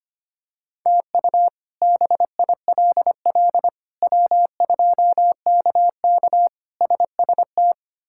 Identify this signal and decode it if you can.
Morse code, heard as a keyed tone, about 25 wpm, 700 Hz. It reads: TU BILL W2KK SST
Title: K1USN SST QSO Practice - 25wpm